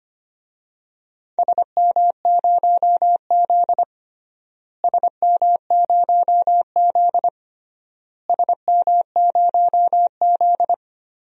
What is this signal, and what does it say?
Morse code, heard as a keyed tone, about 25 wpm, 700 Hz. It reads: SM0Z SM0Z SM0Z
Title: DXCC-100 Call Signs - 4 Characters - Repeated 3x - 25wpm